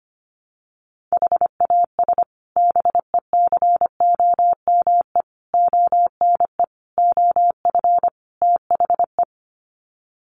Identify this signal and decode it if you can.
Morse code, heard as a keyed tone, about 25 wpm, 700 Hz. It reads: HAS BECOME ONE OF THE
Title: Sets of 5 Words Encore 25wpm